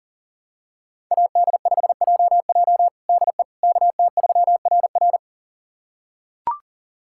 Morse code, heard as a keyed tone, about 40 wpm, 700 Hz. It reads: AD5JJ DE KT3RR
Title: USA Callsign DE Callsign - 5 Characters - 40wpm